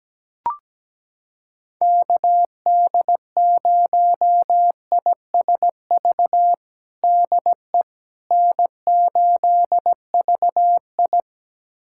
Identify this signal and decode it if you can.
Morse code, heard as a keyed tone, about 17 wpm, 700 Hz. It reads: KD0ISV DE N8VI
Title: USA Callsign DE Callsign 17wpm